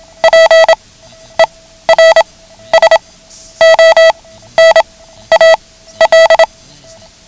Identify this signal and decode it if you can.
Morse code, heard as a keyed tone, about 27 wpm, spaced as for 13 wpm, 650 Hz. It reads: PERSONAL
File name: ham radio / CodeQuick / Code Quick 09